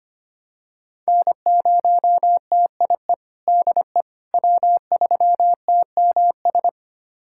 Morse code, heard as a keyed tone, about 25 wpm, 700 Hz. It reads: N0TIE DE W3TMS
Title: USA Callsign DE Callsign - 5 Characters - 25wpm